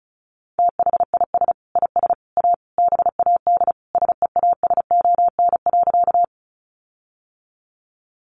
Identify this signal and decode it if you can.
Morse code, heard as a keyed tone, about 34 wpm, 700 Hz. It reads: THIS IS A BAD SEASON.
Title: Sentences from Top 1400 Words 35wpm